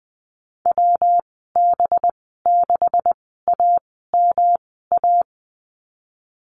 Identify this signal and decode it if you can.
Morse code, heard as a keyed tone, about 20 wpm, spaced as for 14 wpm, 700 Hz. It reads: WB6AMA
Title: 219 Lesson 35g - Number 0 - Call Signs 20wpm